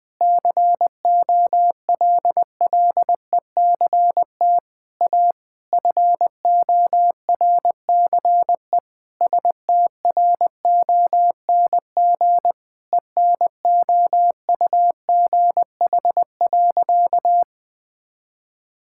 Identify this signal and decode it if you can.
Morse code, heard as a keyed tone, about 20 wpm, 700 Hz. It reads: COLLECT A FORCE STRONG ENOUGH.
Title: Sentences from Top 1600 Words 20wpm